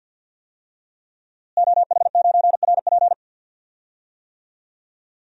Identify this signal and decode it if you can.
Morse code, heard as a keyed tone, about 51 wpm, 700 Hz. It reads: OH9RP
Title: DXCC-100 Call Signs - 5 Characters - Repeated 2x - 50wpm